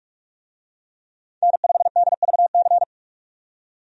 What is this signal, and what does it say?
Morse code, heard as a keyed tone, about 41 wpm, 700 Hz. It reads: N5DVC
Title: USA Call Signs - 5 Characters - 45wpm